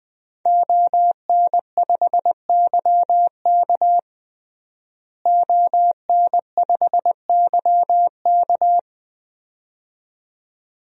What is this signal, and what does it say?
Morse code, heard as a keyed tone, about 20 wpm, 700 Hz. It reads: ON5YK ON5YK
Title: DXCC-100 Call Signs - 5 Characters - Repeated 2x - 20wpm